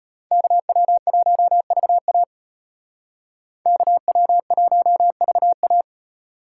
Morse code, heard as a keyed tone, about 33 wpm, 700 Hz. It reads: KW1VA KW1VA